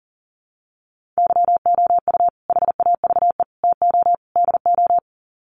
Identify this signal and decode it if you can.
Morse code, heard as a keyed tone, about 40 wpm, 700 Hz. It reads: YOU HAVE TO DO
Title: Sets of 4 Words 40wpm